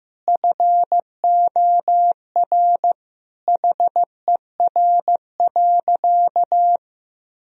Morse code, heard as a keyed tone, about 15 wpm, 700 Hz. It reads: FOR HER.
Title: Sentences from Top 1000 Words - Review Part 4 - 15wpm